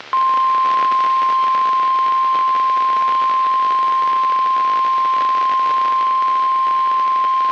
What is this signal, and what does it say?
Emergency Alert System (SAME) tones (the 1050 Hz attention tone). It is heard from 1.0 to 6.5 s.